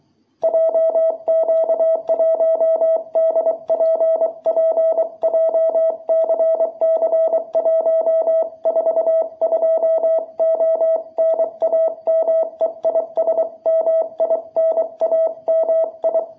Morse code, heard as a keyed tone, about 24 wpm, 660 Hz. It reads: JQ1DPP JCC142O NAME IS MINAMI
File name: jh8cbh / 2021cw